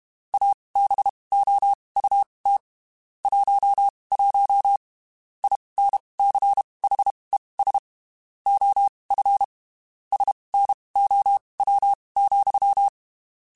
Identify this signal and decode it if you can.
Morse code, heard as a keyed tone, about 32 wpm, spaced as for 23 wpm, 800 Hz. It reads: ABOUT 11 INCHES OF SNOW,
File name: (spaced) / QSOs Spaced-mp3s / WA7QOX 1 @30 sp